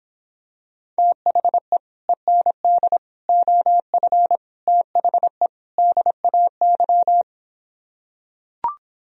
Morse code, heard as a keyed tone, about 26 wpm, 700 Hz. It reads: THE END OF THE DAY